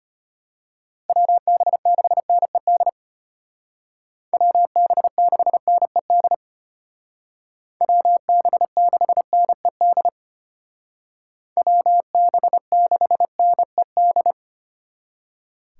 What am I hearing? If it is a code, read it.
Morse code, heard as a keyed tone, about 33 wpm, 700 Hz. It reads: WB6NED WB6NED WB6NED WB6NED